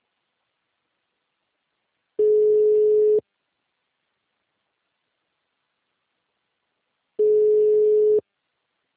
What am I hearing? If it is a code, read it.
Ringback tone (Europe, 425 Hz).